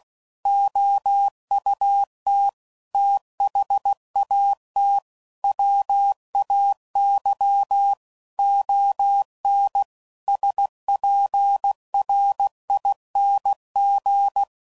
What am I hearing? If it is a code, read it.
Morse code, heard as a keyed tone, about 16 wpm, 800 Hz. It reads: OUT THAT WAY ON SPRING
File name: CW Code Groups @15WPM / QSOs / WA7QOX